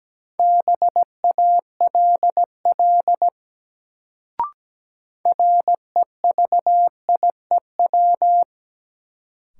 Morse code, heard as a keyed tone, about 17 wpm, 700 Hz. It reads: BALL REVIEW
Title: Sentences from Top 900 Words - New Word Drill - 17wpm